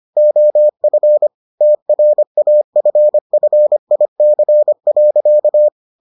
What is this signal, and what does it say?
Morse code, heard as a keyed tone, about 25 wpm, 600 Hz. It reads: OF TRAFFIC.